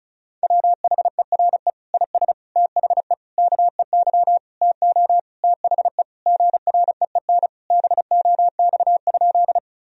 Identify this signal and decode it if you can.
Morse code, heard as a keyed tone, about 34 wpm, 700 Hz. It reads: WHERE IS THE KEY TO THE GREEN BOX?
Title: Sentences from Top 1000 Words - Review Part 4 - 35wpm